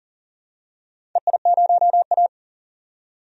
Morse code, heard as a keyed tone, about 40 wpm, 690 Hz. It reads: EI0A
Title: DXCC-100 Call Signs - 4 Characters - 40wpm